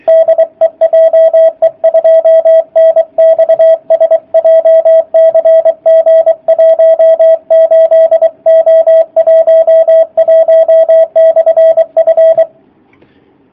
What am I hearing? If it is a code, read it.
Morse code, heard as a keyed tone, about 24 wpm, 650 Hz, with radio noise. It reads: DE JE2NXS JCG 18O11/F
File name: jh8cbh / 2019cw / sp27- w